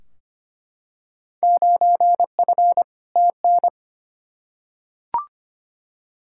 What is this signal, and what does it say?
Morse code, heard as a keyed tone, about 25 wpm, 700 Hz. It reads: 9F TN
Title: Field Day Exchanges Encore - 25wpm